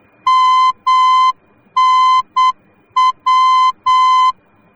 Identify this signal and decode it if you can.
Morse code, heard as a keyed tone, about 8 wpm, 1030 Hz. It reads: MNW
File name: germany / mx / MNW-D